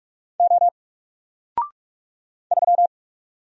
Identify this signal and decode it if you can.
Morse code, heard as a keyed tone, about 46 wpm, 700 Hz. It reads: O 3
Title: Single Letter-Number Alt 45wpm